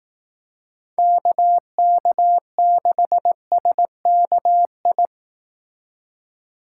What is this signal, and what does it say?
Morse code, heard as a keyed tone, about 18 wpm, 700 Hz. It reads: KK6SKI
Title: Speed Racing - USA Call Signs 15wpm